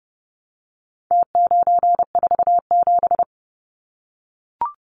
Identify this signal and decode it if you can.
Morse code, heard as a keyed tone, about 30 wpm, 700 Hz. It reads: T947
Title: Numbers - Contest Style - 4-Digit 30wpm